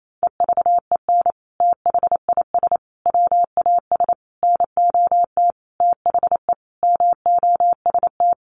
Morse code, heard as a keyed tone, about 28 wpm, 700 Hz. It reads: EVEN THIS WAS NOT THE MOST
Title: Sentences from Top 1100 Words 28wpm